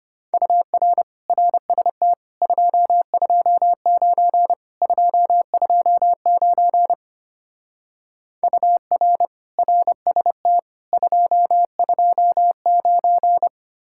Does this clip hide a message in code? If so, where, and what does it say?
Yes: Morse code from 0.3 to 13.5 s, heard as a keyed tone, about 33 wpm, 700 Hz. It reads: UR RST 229 229 UR RST 229